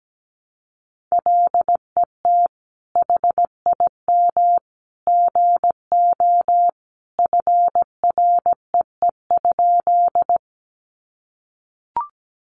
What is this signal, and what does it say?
Morse code, heard as a keyed tone, about 17 wpm, 700 Hz. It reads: LET HIM GO FREE?
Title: Sentences from Top 500 Words 17wpm